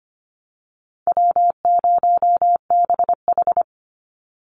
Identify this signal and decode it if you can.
Morse code, heard as a keyed tone, about 25 wpm, 700 Hz. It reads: W0BH
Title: Call Signs - 25wpm